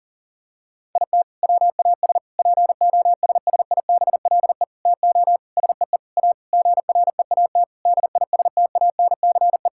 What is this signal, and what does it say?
Morse code, heard as a keyed tone, about 37 wpm, 700 Hz. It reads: IT WAS POSSIBLE TO SEE A GREAT DISTANCE
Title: Sentences from Top 1600 Words 40wpm